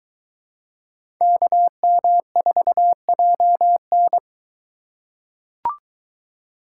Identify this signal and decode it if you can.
Morse code, heard as a keyed tone, about 23 wpm, 700 Hz. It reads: KM4JN